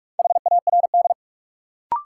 Morse code, heard as a keyed tone, about 42 wpm, 700 Hz. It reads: HARD
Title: Sentences from Top 300 Words - New Word Drill - 45wpm